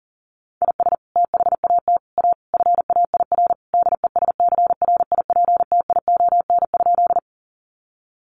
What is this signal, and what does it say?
Morse code, heard as a keyed tone, about 40 wpm, 700 Hz. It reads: IS THAT A FAIR DESCRIPTION?
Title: Sentences from Top 1300 Words 40wpm